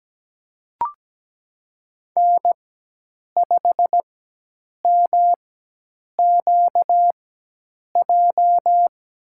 Morse code, heard as a keyed tone, about 17 wpm, spaced as for 8 wpm, 700 Hz. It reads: N5MQJ
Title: USA Call Signs - 5 Characters - 4x Character Spacing - 17wpm